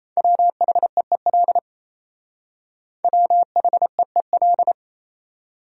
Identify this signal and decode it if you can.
Morse code, heard as a keyed tone, about 35 wpm, 700 Hz. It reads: WHEEL WHEEL
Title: Speed-Racing Top 500 Words 28wpm